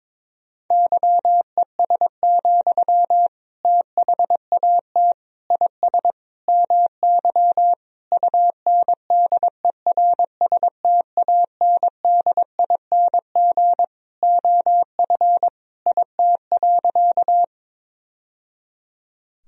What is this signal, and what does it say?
Morse code, heard as a keyed tone, about 22 wpm, 700 Hz. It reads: YES, THAT IS MY UNDERSTANDING OF IT.
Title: Sentences from Top 1000 Words - Review Part 3 - 22wpm